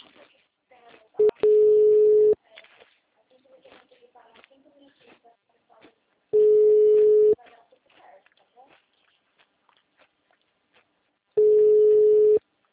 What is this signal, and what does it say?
Ringback tone (Europe, 425 Hz), with speech.
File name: out- -2013- - - .8614